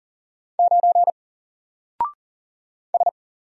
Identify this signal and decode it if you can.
Morse code, heard as a keyed tone, about 40 wpm, 700 Hz. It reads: 9 S
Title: Single Letter-Number Alt 40wpm